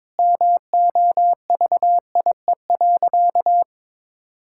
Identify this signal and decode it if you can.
Morse code, heard as a keyed tone, about 22 wpm, 700 Hz. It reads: MOVIE.
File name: practice / 22wpm / Sentences with 4-Word Phrases - 22wpm